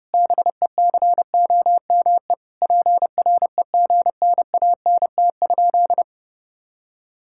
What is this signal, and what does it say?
Morse code, heard as a keyed tone, about 28 wpm, 700 Hz. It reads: BECOME PREGNANT?
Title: Sentences from Top 1000 Words - Review Part 6 - 30wpm